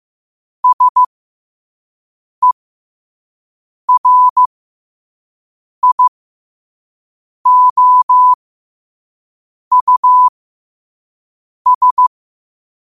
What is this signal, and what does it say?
Morse code, heard as a keyed tone, about 15 wpm, spaced as for 5 wpm, 1000 Hz. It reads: SERIOUS
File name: qotdfiles / 5wpm